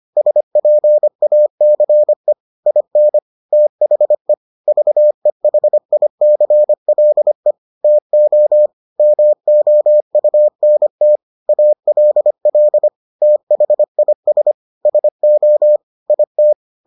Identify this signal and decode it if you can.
Morse code, heard as a keyed tone, about 25 wpm, 600 Hz. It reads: SPACE IN THE VEHICLE TO MOUNT ALL THIS SO IT